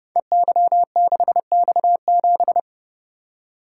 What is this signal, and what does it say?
Morse code, heard as a keyed tone, about 30 wpm, 700 Hz. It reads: EY6X7
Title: Code Groups - 5 Letter:Numbers - 30wpm